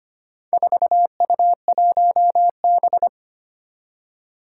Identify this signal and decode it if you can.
Morse code, heard as a keyed tone, about 25 wpm, 700 Hz. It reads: 4U1B
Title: Four Challenging Letter-Number Combinations 25wpm